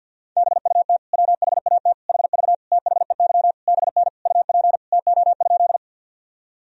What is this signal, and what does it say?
Morse code, heard as a keyed tone, about 51 wpm, 700 Hz. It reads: BUT WHAT HV THEY BN UP TO?
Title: Sentences from Top 100 Words and Abbreviations 50wpm